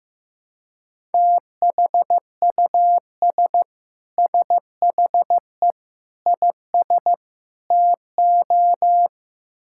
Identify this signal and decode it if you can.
Morse code, heard as a keyed tone, about 15 wpm, 700 Hz. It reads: THUS SHE IS TO
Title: Sentences from Top 1000 Words - Review Part 5 - 15wpm